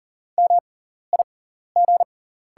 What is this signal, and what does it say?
Morse code, heard as a keyed tone, about 40 wpm, spaced as for 14 wpm, 700 Hz. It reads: MIG
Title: 6x Character Spacing - Top 300 Words - 40wpm